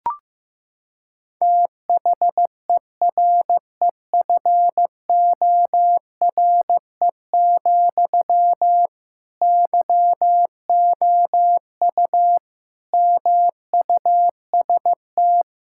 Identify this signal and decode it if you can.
Morse code, heard as a keyed tone, about 15 wpm, 700 Hz. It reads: THEREFORE, YOU MUST
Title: Sentences from Top 1000 Words - Review Part 5 - 15wpm